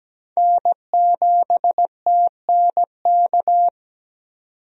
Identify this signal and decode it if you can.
Morse code, heard as a keyed tone, about 17 wpm, 700 Hz. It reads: N7TNK
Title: USA Call Signs - 5 Characters - 17wpm